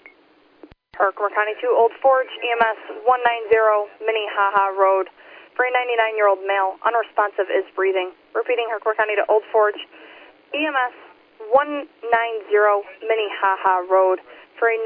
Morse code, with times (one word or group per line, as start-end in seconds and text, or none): none